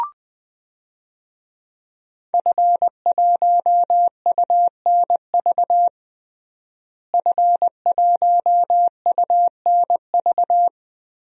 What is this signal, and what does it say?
Morse code, heard as a keyed tone, about 21 wpm, 700 Hz. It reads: F1UNV F1UNV
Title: DXCC-100 Call Signs - 5 Characters - Repeated 3x - 20wpm